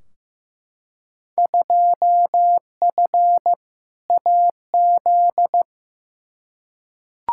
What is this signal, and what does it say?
Morse code, heard as a keyed tone, about 15 wpm, 700 Hz. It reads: 2F AZ
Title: Field Day Exchanges - 15wpm